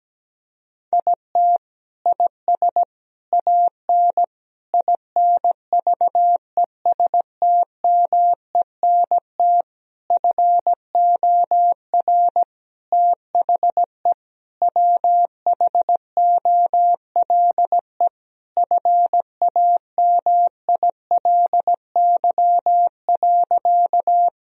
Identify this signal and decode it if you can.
Morse code, heard as a keyed tone, about 17 wpm, 700 Hz. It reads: IT IS AN INVESTMENT FOR THE WHOLE FAMILY.
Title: Sentences from Top 1000 Words - Review Part 4 - 17wpm